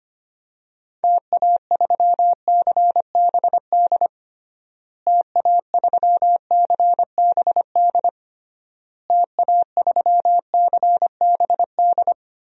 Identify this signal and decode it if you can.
Morse code, heard as a keyed tone, about 25 wpm, 700 Hz. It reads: TA3CBD TA3CBD TA3CBD
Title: DXCC-100 Call Signs - 6 Characters - Repeated 3x - 25wpm